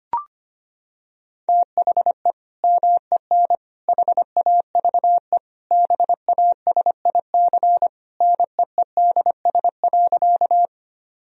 Morse code, heard as a keyed tone, about 25 wpm, 700 Hz. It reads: THE MEN HAVE BASIC NEEDS.
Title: Sentences from Top 1000 Words - Review Part 3 - 25wpm